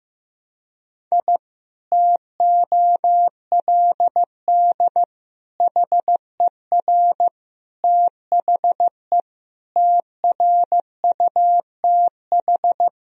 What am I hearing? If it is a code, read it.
Morse code, heard as a keyed tone, about 15 wpm, 700 Hz. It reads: I TOLD HER THE TRUTH
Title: Sentences from Top 1000 Words - Review Part 3 - 15wpm